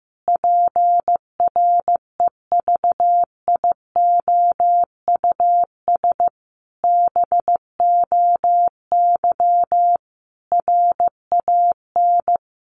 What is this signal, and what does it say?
Morse code, heard as a keyed tone, about 15 wpm, 700 Hz. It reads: PREVIOUS BOY RAN
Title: Sentences from Top 1500 Words 15wpm